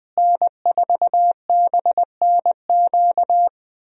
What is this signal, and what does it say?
Morse code, heard as a keyed tone, about 20 wpm, 700 Hz. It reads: N4BNQ
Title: USA Call Signs - 5 Characters - Repeated 2x - 20wpm